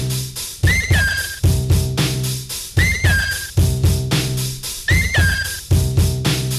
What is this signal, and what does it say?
Emergency Alert System (SAME) tones (header bursts), heard as harsh buzzing bursts. It is heard from 0.7 to 5.6 s.